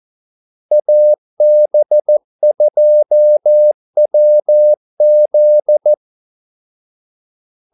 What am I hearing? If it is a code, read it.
Morse code, heard as a keyed tone, about 14 wpm, 600 Hz. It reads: AB2WZ
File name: Callsigns-14wpm